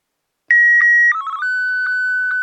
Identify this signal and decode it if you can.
SSTV (slow-scan television), Martin 1.